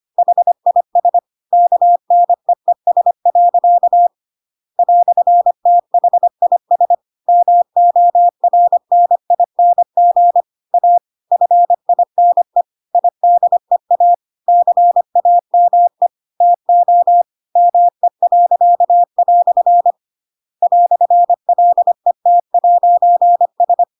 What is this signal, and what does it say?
Morse code, heard as a keyed tone, about 25 wpm, 700 Hz. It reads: HIS KNEES. "THIS MORNING A FINE IDEA CAME TO ME." "LET'S
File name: Pinocchio / Pinocchio ch2 25 25 ik1hge